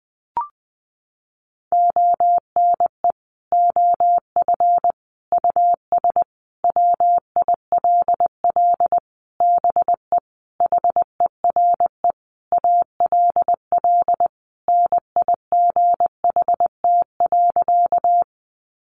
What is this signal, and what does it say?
Morse code, heard as a keyed tone, about 20 wpm, 700 Hz. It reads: ONE OF US WILL BE HERE ALL NIGHT.
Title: Sentences from Top 300 Words 20wpm